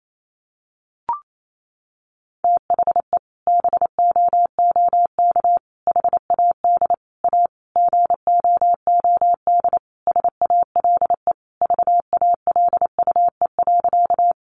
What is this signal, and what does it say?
Morse code, heard as a keyed tone, about 28 wpm, 700 Hz. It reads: THE BOOK HAD A GOOD SALE VALUE.
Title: Sentences from Top 1400 Words 28wpm